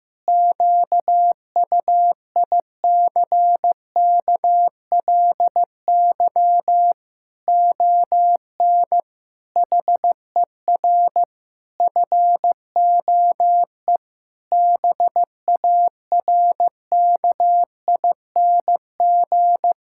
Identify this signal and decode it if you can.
Morse code, heard as a keyed tone, about 15 wpm, 700 Hz. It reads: QUICKLY ON HER FOE BARKING
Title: Pangram Challenge v2 - 15wpm